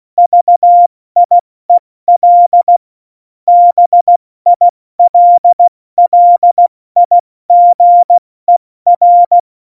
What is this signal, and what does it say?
Morse code, heard as a keyed tone, about 16 wpm, 700 Hz. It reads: VIEL BILLIGER
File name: Morsekurs / Witze / Witz 07 Juli 2020, 70 F80